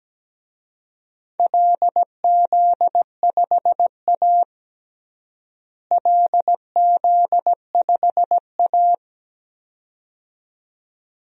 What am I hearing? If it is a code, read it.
Morse code, heard as a keyed tone, about 17 wpm, 700 Hz. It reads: LZ5A LZ5A
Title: DXCC-100 Call Signs - 4 Characters - Repeated 2x - 17wpm